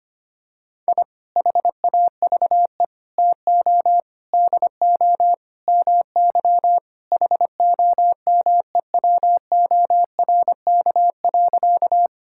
Morse code, heard as a keyed tone, about 25 wpm, 700 Hz. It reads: I HAVE TO DO MY HOMEWORK.
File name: practice / 25wpm / Sentences with 3-Word Phrases - 25wpm